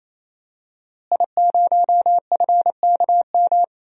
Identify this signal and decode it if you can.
Morse code, heard as a keyed tone, about 28 wpm, 700 Hz. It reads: I0FKM
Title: DXCC-100 Call Signs - 5 Characters - Repeated 3x - 28wpm